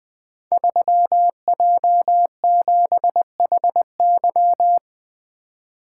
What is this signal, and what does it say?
Morse code, heard as a keyed tone, about 20 wpm, 700 Hz. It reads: 3J7HY
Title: Code Groups - 5 Letter:Numbers - 20wpm